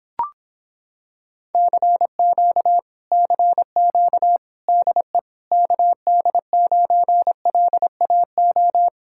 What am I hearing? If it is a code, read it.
Morse code, heard as a keyed tone, about 26 wpm, 700 Hz. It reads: CQ CQ DE KD9LAO